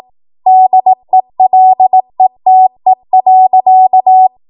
Morse code, heard as a keyed tone, about 18 wpm, 750 Hz. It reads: DELETE.
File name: Morse / 18WPM